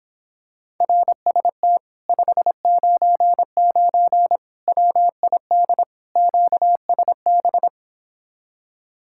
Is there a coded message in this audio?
Yes: Morse code from 0.8 to 7.7 s, heard as a keyed tone, about 26 wpm, 700 Hz. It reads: RST 599 WID QSB